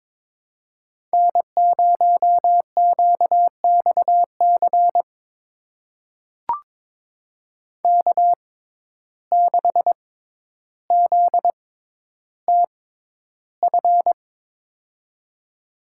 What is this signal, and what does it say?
Morse code, heard as a keyed tone, about 21 wpm, spaced as for 11 wpm, 700 Hz. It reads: N0QXC K6ZTF
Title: USA Call Signs - 5 Characters - 6x Character Spacing - 22wpm